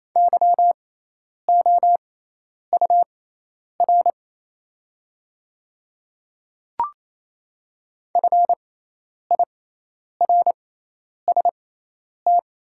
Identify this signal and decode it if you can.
Morse code, heard as a keyed tone, about 28 wpm, spaced as for 10 wpm, 700 Hz. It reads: YOUR FIRST
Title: 6x Character Spacing - Top 100 Words - 28wpm